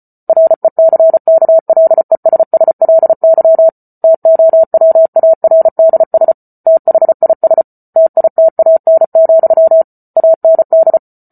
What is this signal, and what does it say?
Morse code, heard as a keyed tone, about 34 wpm, 650 Hz. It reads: RECKLESSLY TOWARDS THIS TITAN, AND